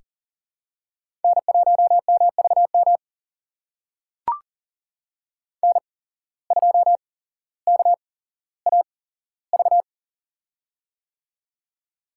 Morse code, heard as a keyed tone, about 42 wpm, spaced as for 17 wpm, 700 Hz. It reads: N1MVM N2KAV
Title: USA Call Signs - 5 Characters - 8x Character Spacing - 40wpm